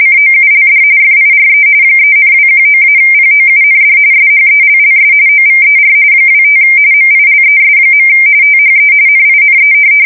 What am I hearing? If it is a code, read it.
RTTY, about 45.45 baud. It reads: COULD FIRE 3,219-POUND SHELLS OVER 25 MILES. THE NINE
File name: fsk170 / USS-New-Jersey